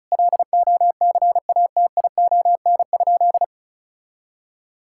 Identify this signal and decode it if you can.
Morse code, heard as a keyed tone, about 36 wpm, 700 Hz. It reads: LOCATION?
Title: Sentences from Top 1000 Words - Review Part 6 - 35wpm